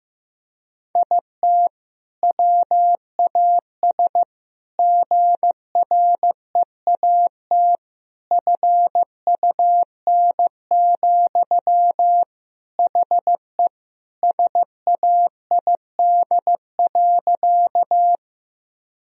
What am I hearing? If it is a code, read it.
Morse code, heard as a keyed tone, about 15 wpm, 700 Hz. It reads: IT WAS GREAT FUN, HE SAID.
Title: Sentences from Top 1000 Words - Review Part 4 - 15wpm